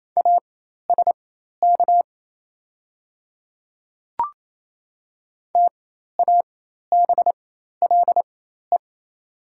Morse code, heard as a keyed tone, about 28 wpm, spaced as for 13 wpm, 700 Hz. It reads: ASK TABLE
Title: 4x Character Spacing - Top 500 Words - 28wpm